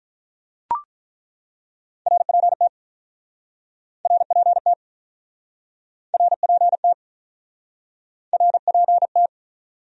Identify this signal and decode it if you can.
Morse code, heard as a keyed tone, about 48 wpm, 700 Hz. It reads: RPT RPT RPT RPT